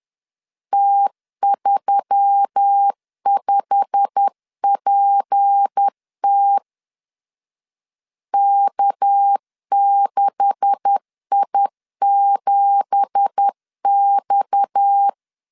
Morse code, heard as a keyed tone, about 10 wpm, 800 Hz. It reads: T35PT K6I7X
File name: CW / DET / 1 9 1500C S B 10wpm 4439